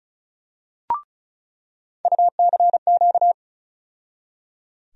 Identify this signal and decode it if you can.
Morse code, heard as a keyed tone, about 36 wpm, 700 Hz. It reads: UCQ
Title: USA Call Sign Postfixes - 3 Letters 35wpm